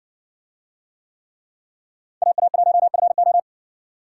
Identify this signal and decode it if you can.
Morse code, heard as a keyed tone, about 60 wpm, 700 Hz. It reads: AA1FO